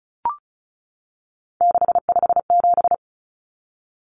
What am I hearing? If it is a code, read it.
Morse code, heard as a keyed tone, about 34 wpm, 700 Hz. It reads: 657
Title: Common Cut Numbers - 3-Digit 35wpm